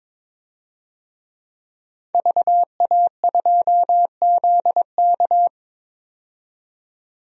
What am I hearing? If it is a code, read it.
Morse code, heard as a keyed tone, about 22 wpm, 700 Hz. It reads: VA2ZK
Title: DXCC-100 Call Signs - 5 Characters - 22wpm